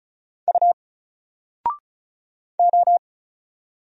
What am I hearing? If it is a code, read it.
Morse code, heard as a keyed tone, about 33 wpm, 700 Hz. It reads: U O
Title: Single Letter-Number 35wpm